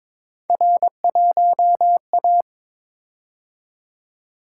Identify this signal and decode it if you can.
Morse code, heard as a keyed tone, about 23 wpm, 700 Hz. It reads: R1A
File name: practice / 22wpm / Three Letter-Number Combinations - 22wpm